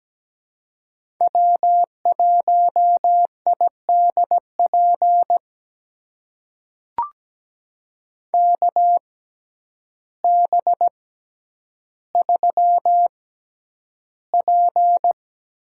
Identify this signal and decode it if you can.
Morse code, heard as a keyed tone, about 17 wpm, spaced as for 9 wpm, 700 Hz. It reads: W1IDP KB3P
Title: USA Call Signs - 6x Character Spacing - 17wpm